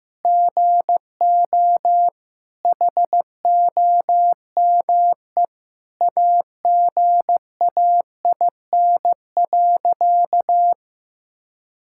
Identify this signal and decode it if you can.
Morse code, heard as a keyed tone, about 15 wpm, 700 Hz. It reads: GO HOME AGAIN.
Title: Sentences from Top 1000 Words - Review Part 6 - 15wpm